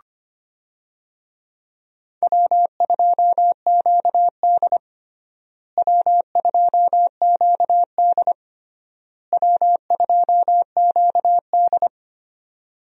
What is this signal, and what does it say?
Morse code, heard as a keyed tone, about 25 wpm, 700 Hz. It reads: W2QD W2QD W2QD
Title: USA Call Signs - 4 Characters - Repeated 3x - 25wpm